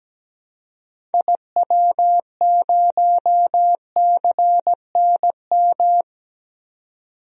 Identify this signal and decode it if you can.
Morse code, heard as a keyed tone, about 17 wpm, 700 Hz. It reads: IW0CNM
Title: DXCC-100 Call Signs - 6 Characters - 17wpm